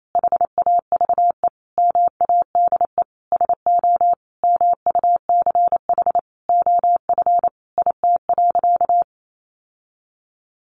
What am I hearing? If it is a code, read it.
Morse code, heard as a keyed tone, about 28 wpm, 700 Hz. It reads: HAVE MADE SO MUCH OF IT.
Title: Sentences from Top 1400 Words 28wpm